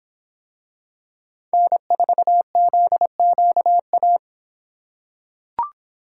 Morse code, heard as a keyed tone, about 26 wpm, 700 Hz. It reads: N4ZQA